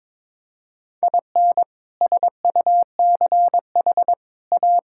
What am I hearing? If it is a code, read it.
Morse code, heard as a keyed tone, about 22 wpm, 700 Hz. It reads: IN SUCH A
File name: practice / 22wpm / Sets of 6 Words - 22wpm